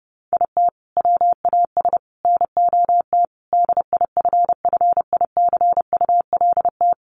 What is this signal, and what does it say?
Morse code, heard as a keyed tone, about 30 wpm, 700 Hz. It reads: IT WAS NOT DIFFICULT
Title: Sentences from Top 1200 Words 30wpm